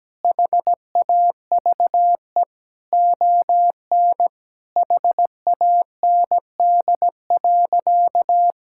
Morse code, heard as a keyed tone, about 17 wpm, 700 Hz. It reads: HAVE ON HAND.
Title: Sentences from Top 1000 Words - Review Part 6 - 17wpm